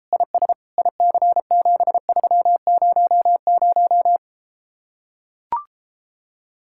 Morse code, heard as a keyed tone, about 33 wpm, 700 Hz. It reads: IS IC7300